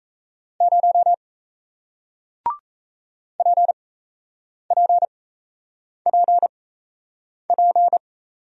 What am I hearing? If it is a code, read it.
Morse code, heard as a keyed tone, about 38 wpm, 700 Hz. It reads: 0 P P P P